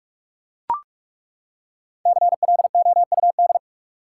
Morse code, heard as a keyed tone, about 42 wpm, 700 Hz. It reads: CLOUD
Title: Five Letter Words - 45wpm